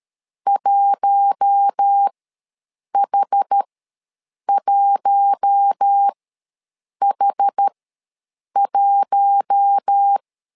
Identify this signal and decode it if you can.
Morse code, heard as a keyed tone, about 13 wpm, spaced as for 7 wpm, 800 Hz. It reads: 1H1H1
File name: LEZIONI / 4 Q1 ws 12wpm 2129